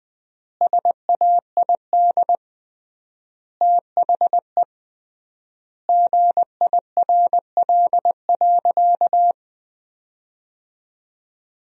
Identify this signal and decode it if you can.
Morse code, heard as a keyed tone, about 20 wpm, 700 Hz. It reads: SAID THE GIRL.
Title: 3x Word-Spacing - Sentences from Top 400 Words - 20wpm